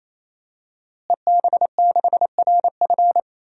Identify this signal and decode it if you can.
Morse code, heard as a keyed tone, about 28 wpm, 700 Hz. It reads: EB6RF
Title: DXCC-100 Call Signs - 5 Characters - Repeated 3x - 28wpm